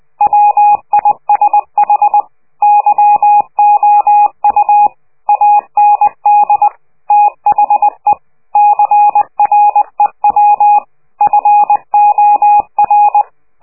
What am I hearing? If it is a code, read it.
Morse code, heard as a keyed tone, about 20 wpm, 830 Hz. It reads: WISH YOU AND THE CREW FOR
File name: worldcoastal / scheveningenradio / morsepch